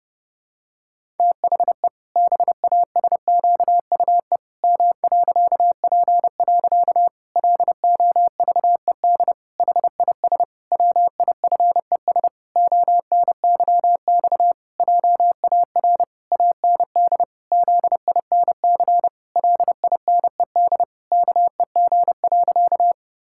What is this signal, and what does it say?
Morse code, heard as a keyed tone, about 30 wpm, 700 Hz. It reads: THE BASQUE M.P. LOVED HIS WIFES ONYX JAR AND ZINC LINED KEG.
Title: Pangram Challenge v2 - 30wpm